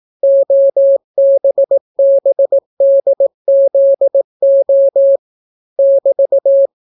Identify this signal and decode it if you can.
Morse code, heard as a keyed tone, about 18 wpm, 550 Hz. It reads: OBBDZO <BT>